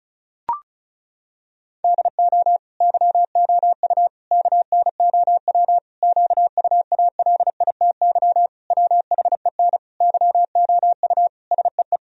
Morse code, heard as a keyed tone, about 36 wpm, 700 Hz. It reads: DO YOU KNOW QUALITY WHEN YOU SEE
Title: Sentences from Top 1000 Words - Review Part 4 - 35wpm